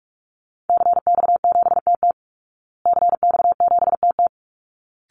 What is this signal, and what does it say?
Morse code, heard as a keyed tone, about 44 wpm, 700 Hz. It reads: CX7TT CX7TT
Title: Call Signs Repeated 3x - 45wpm